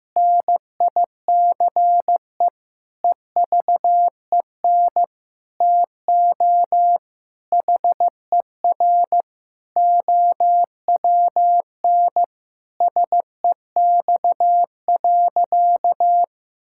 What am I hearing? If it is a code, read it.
Morse code, heard as a keyed tone, about 16 wpm, 700 Hz. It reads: NICE EVEN TO HER OWN SEX.
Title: Sentences from Top 1600 Words 15wpm